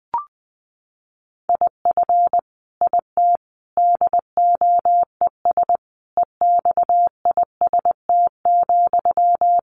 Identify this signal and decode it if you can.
Morse code, heard as a keyed tone, about 20 wpm, 700 Hz. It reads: IF IT DOES EXIST,
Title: Sentences from Top 1300 Words 20wpm